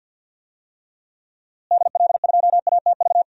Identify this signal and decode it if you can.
Morse code, heard as a keyed tone, about 52 wpm, 700 Hz. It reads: DL2ATV
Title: DXCC-100 Call Signs - 6 Characters - Repeated 3x - 50wpm